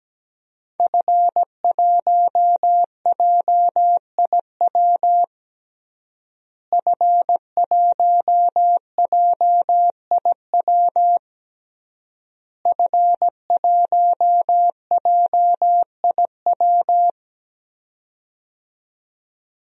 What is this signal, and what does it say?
Morse code, heard as a keyed tone, about 17 wpm, 700 Hz. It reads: F1JIW F1JIW F1JIW
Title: DXCC-100 Call Signs - 5 Characters - Repeated 3x - 17wpm